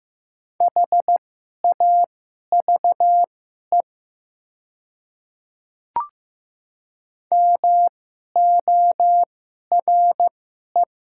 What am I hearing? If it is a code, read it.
Morse code, heard as a keyed tone, about 15 wpm, spaced as for 11 wpm, 700 Hz. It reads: HAVE MORE
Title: 2x Character Spacing - Top 300 Words - 15wpm